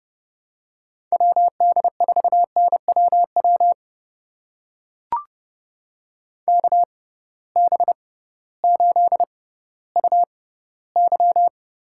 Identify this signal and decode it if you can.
Morse code, heard as a keyed tone, about 30 wpm, spaced as for 16 wpm, 700 Hz. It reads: WD4NWW KB8UY
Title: USA Call Signs - 6 Characters - 6x Character Spacing - 30wpm